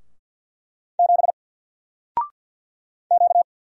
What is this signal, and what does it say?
Morse code, heard as a keyed tone, about 48 wpm, 700 Hz. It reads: 7 Q
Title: Single Letter-Number Alt 50wpm